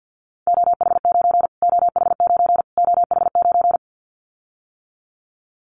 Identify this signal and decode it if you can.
Morse code, heard as a keyed tone, about 50 wpm, 700 Hz. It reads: OH9 OH9 OH9
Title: Callsign Prefixes for Top 25 DXCC - Repeated 3x - 50wpm